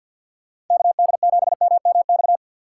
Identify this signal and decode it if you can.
Morse code, heard as a keyed tone, about 47 wpm, 700 Hz. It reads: KD7MMX
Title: USA Call Signs - 6 Characters - 6x Character Spacing - 50wpm